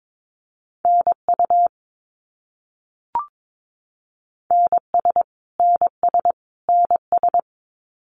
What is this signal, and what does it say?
Morse code, heard as a keyed tone, about 22 wpm, 700 Hz. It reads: NU NS NS NS
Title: Canadian Provinces and Territories - Repeated 3x - 22wpm